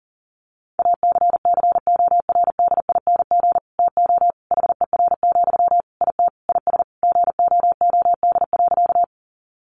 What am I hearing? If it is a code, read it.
Morse code, heard as a keyed tone, about 40 wpm, 700 Hz. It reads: ACCORDING TO HER, IT IS GOOD.
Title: Sentences from Top 800 Words 40wpm